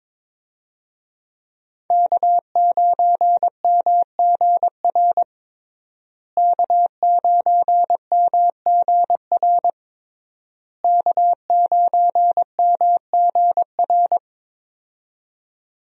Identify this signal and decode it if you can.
Morse code, heard as a keyed tone, about 22 wpm, 700 Hz. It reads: K9MGR K9MGR K9MGR
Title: USA Call Signs - 5 Characters - Repeated 3x - 22wpm